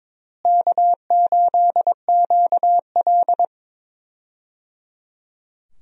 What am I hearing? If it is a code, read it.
Morse code, heard as a keyed tone, about 21 wpm, 700 Hz. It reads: K8QL
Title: USA Call Signs - 4 Characters - Repeated 3x - 22wpm